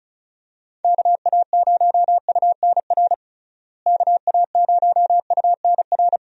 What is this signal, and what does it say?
Morse code, heard as a keyed tone, about 36 wpm, 700 Hz. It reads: KA0UNR KA0UNR
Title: USA Call Signs - 6 Characters - Repeated 2x - 35wpm